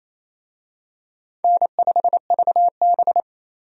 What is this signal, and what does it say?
Morse code, heard as a keyed tone, about 28 wpm, 700 Hz. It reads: N5VB
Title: USA Call Signs - 4 Characters - 28wpm